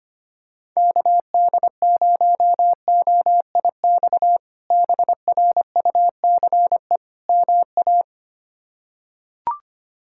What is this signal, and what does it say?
Morse code, heard as a keyed tone, about 25 wpm, 700 Hz. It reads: KD0OIX BRUCE MA
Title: K1USN SST QSO Practice - 25wpm